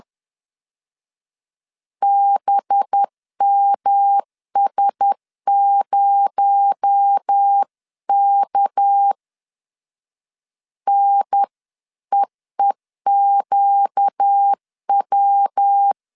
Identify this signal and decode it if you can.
Morse code, heard as a keyed tone, about 10 wpm, 800 Hz. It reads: BMS0K NEEQW
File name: CW / DET / 1 6 1500C S B 10wpm 4137